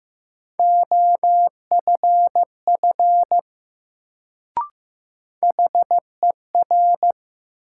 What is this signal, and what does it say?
Morse code, heard as a keyed tone, about 15 wpm, 700 Hz. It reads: OFF HER
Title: Sets of 2 Words - New Word Drill - 15wpm